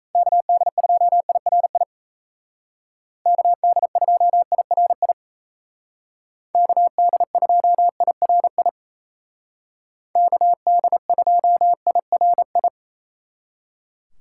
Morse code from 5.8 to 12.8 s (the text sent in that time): KD2IRI KD2IRI